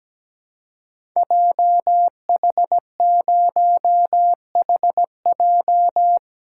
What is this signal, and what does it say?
Morse code, heard as a keyed tone, about 17 wpm, 700 Hz. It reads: JH0HJ
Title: Code Groups - 5 Letter:Numbers - 17wpm